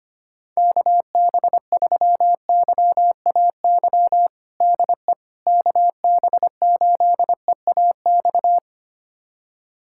Morse code, heard as a keyed tone, about 25 wpm, 700 Hz. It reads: KB3YAY DE KB8EAX
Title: USA Callsign DE Callsign - 6 Characters - 25wpm